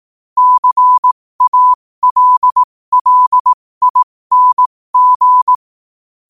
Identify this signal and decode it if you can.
Morse code, heard as a keyed tone, about 18 wpm, spaced as for 15 wpm, 1000 Hz. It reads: CALLING
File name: qotdfiles / 15wpm